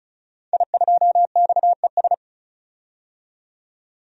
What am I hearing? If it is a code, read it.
Morse code, heard as a keyed tone, about 34 wpm, 700 Hz. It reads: I2XES
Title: DXCC-100 Call Signs - 5 Characters - 35wpm